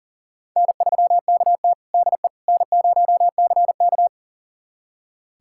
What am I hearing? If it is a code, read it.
Morse code, heard as a keyed tone, about 40 wpm, 700 Hz. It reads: N3KT DE N0CK
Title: USA Callsign DE Callsign - 4 Characters - 40wpm